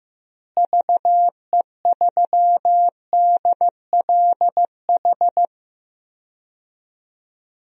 Morse code, heard as a keyed tone, about 15 wpm, 700 Hz. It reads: VE3DLH
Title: DXCC-100 Call Signs - 6 Characters - Repeated 2x - 15wpm